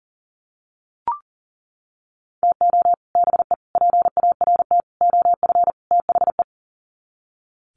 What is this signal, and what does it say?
Morse code, heard as a keyed tone, about 40 wpm, 700 Hz. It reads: TO BE PART OF THE
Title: Sets of 5 Words 40wpm